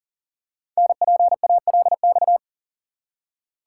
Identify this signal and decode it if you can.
Morse code, heard as a keyed tone, about 39 wpm, 700 Hz. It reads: NPALX
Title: Code Groups - 5 Letter:Numbers - 40wpm